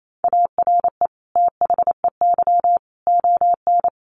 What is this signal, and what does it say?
Morse code, heard as a keyed tone, about 28 wpm, 700 Hz. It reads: ARE THEY ON
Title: Sentences from Top 500 Words 28wpm